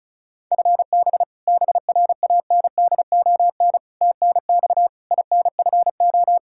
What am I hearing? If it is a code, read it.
Morse code, heard as a keyed tone, about 34 wpm, 700 Hz. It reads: FB BRANDON TNX INFO
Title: Practice QSO - Exchange 3 - More Info - 35wpm